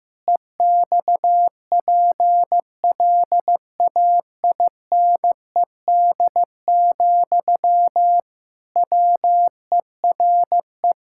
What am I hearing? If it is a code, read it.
Morse code, heard as a keyed tone, about 15 wpm, 700 Hz. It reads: EXPLAINED, WERE
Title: Pangram Challenge v2 - 15wpm